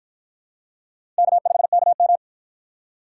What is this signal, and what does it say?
Morse code, heard as a keyed tone, about 53 wpm, 700 Hz. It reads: K5KM